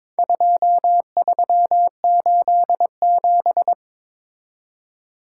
Morse code, heard as a keyed tone, about 22 wpm, 700 Hz. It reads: 2387
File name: practice / 22wpm / Four Letter-Number Combinations - 22wpm